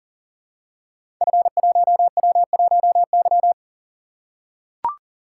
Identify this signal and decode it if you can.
Morse code, heard as a keyed tone, about 41 wpm, 700 Hz. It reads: F1W1Y
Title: Code Groups - 5 Letter:Numbers - 40wpm